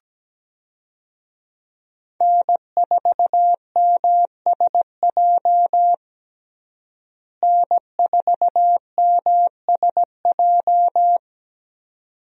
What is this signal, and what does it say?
Morse code, heard as a keyed tone, about 17 wpm, 700 Hz. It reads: N4MSJ N4MSJ
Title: USA Call Signs - 5 Characters - Repeated 2x - 17wpm